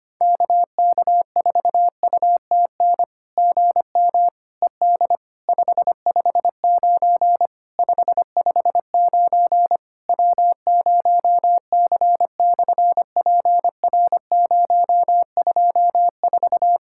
Morse code, heard as a keyed tone, about 24 wpm, 700 Hz. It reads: KK4UTN GM ED 559 559 W0C/PR024
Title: SOTA QSO Practice 25wpm